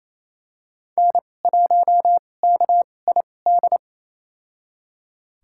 Morse code, heard as a keyed tone, about 28 wpm, spaced as for 20 wpm, 700 Hz. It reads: N1KID
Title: USA Call Signs - 5 Characters - 2x Character Spacing - 28wpm